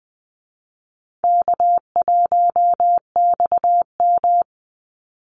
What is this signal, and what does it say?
Morse code, heard as a keyed tone, about 20 wpm, 700 Hz. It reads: K1XM
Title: Call Signs Repeated 2x - 20wpm